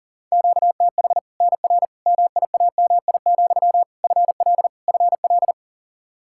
Morse code, heard as a keyed tone, about 40 wpm, 700 Hz. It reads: QTH NR MIAMI, FL FL
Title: Practice QSO - Exchange 1 - Beginning - 40wpm